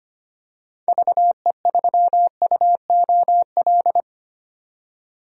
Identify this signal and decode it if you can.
Morse code, heard as a keyed tone, about 25 wpm, 700 Hz. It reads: VE3UOL
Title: DXCC-100 Call Signs - 6 Characters - 25wpm